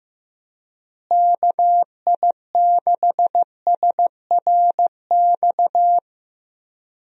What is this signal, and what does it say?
Morse code, heard as a keyed tone, about 15 wpm, 700 Hz. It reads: KI6SRX
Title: USA Call Signs - 6 Characters - Repeated 2x - 15wpm